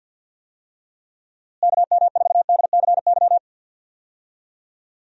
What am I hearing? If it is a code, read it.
Morse code, heard as a keyed tone, about 52 wpm, 700 Hz. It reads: KM4DCY
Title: USA Call Signs - 6 Characters - Repeated 2x - 50wpm